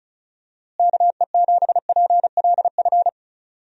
Morse code, heard as a keyed tone, about 34 wpm, 700 Hz. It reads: KE7PLF
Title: USA Call Signs - 6 Characters - Repeated 3x - 35wpm